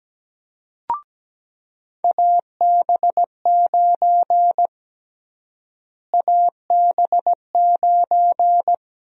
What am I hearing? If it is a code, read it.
Morse code, heard as a keyed tone, about 17 wpm, 700 Hz. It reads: AB9 AB9
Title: USA Call Sign Prefixes - Repeated 2x - 17wpm